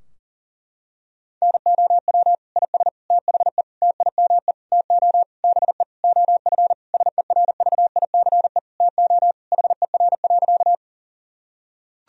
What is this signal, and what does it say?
Morse code, heard as a keyed tone, about 40 wpm, 700 Hz. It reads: NOW IS THE TIME TO BE OF SERVICE TO HER.
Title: Sentences from Top 1000 Words - Review Part 4 - 40wpm